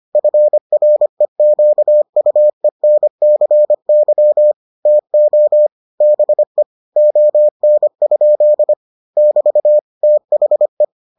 Morse code, heard as a keyed tone, about 25 wpm, 600 Hz. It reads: FREQUENCY TO BE ON? <BT> THE